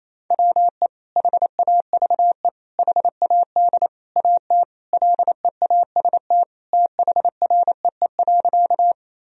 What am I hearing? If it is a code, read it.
Morse code, heard as a keyed tone, about 28 wpm, 700 Hz. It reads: WE HAVE HAD AT LEAST THREE.
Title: Sentences from Top 1000 Words - Review Part 1 - 28wpm